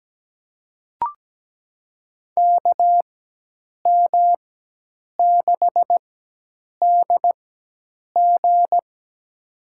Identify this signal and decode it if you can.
Morse code, heard as a keyed tone, about 17 wpm, spaced as for 8 wpm, 700 Hz. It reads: KM6DG
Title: USA Call Signs - 5 Characters - 4x Character Spacing - 17wpm